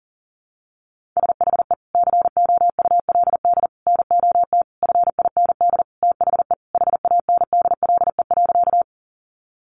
Morse code, heard as a keyed tone, about 37 wpm, 700 Hz. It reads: SHE COULD NOT FIND THE HANDLE.
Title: Sentences from Top 1100 Words 40wpm